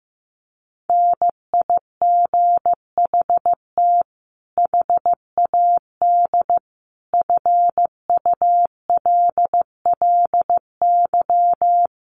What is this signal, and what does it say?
Morse code, heard as a keyed tone, about 15 wpm, 700 Hz. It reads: NIGHT HAD FULLY
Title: Sentences from Top 1300 Words 15wpm